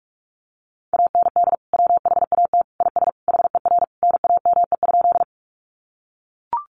Morse code, heard as a keyed tone, about 44 wpm, 700 Hz. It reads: AND WHAT IS HER NAME?
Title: Sentences from Top 200 Words 45wpm